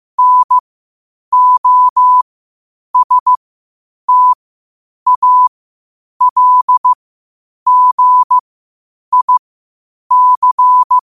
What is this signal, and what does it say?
Morse code, heard as a keyed tone, about 15 wpm, spaced as for 8 wpm, 1000 Hz. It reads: NOSTALGIC
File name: qotdfiles / 8wpm